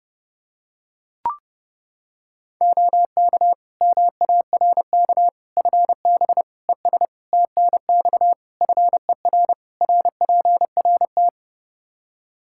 Morse code, heard as a keyed tone, about 30 wpm, 700 Hz. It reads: OK MARK FB ES TNX FER RPRT
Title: Practice QSO - Exchange 2 - Info - 30wpm